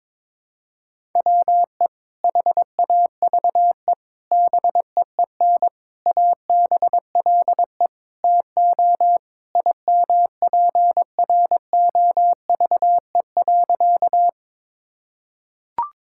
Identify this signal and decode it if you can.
Morse code, heard as a keyed tone, about 22 wpm, 700 Hz. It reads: WE HAVE BEEN ABLE TO IMPROVE.
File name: practice / 22wpm / Sentences with 4-Word Phrases - 22wpm